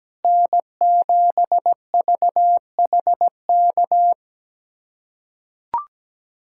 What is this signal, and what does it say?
Morse code, heard as a keyed tone, about 17 wpm, 700 Hz. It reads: N7VHK
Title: USA Callsign DE Callsign 17wpm